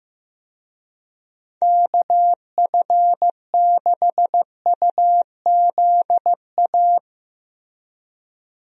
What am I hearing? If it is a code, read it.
Morse code, heard as a keyed tone, about 15 wpm, 700 Hz. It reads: KF6UZA
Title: USA Call Signs - 6 Characters - Repeated 3x - 15wpm